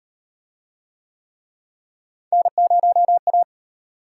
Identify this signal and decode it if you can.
Morse code, heard as a keyed tone, about 37 wpm, 700 Hz. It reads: N0A